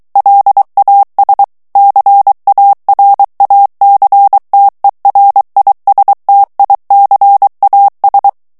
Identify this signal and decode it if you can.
Morse code, heard as a keyed tone, about 23 wpm, 800 Hz. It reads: LAS CARACTERISTICAS
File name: morse mp3 / ESPAGNOL TEXTE EN CLAIR 2 VITESSE 22 MOTS MINUTES